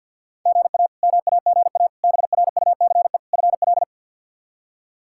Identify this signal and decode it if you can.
Morse code, heard as a keyed tone, about 49 wpm, 700 Hz. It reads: GA MAGA BRUCE FL
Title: K1USN SST QSO Practice - 50wpm